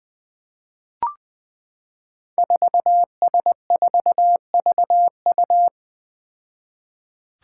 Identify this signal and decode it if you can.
Morse code, heard as a keyed tone, about 20 wpm, 700 Hz. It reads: 4S4VU
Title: Five Challenging Letter-Number Combinations 20wpm